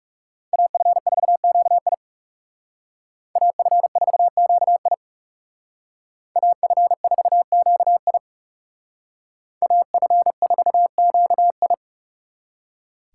Morse code, heard as a keyed tone, about 40 wpm, 700 Hz. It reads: AF4QI AF4QI AF4QI AF4QI